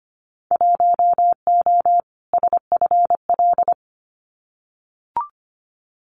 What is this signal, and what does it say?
Morse code, heard as a keyed tone, about 25 wpm, 700 Hz. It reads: 1O SFL
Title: Winter Field Day Exchanges 25wpm